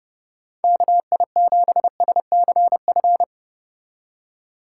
Morse code, heard as a keyed tone, about 30 wpm, 700 Hz. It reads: KI7SCF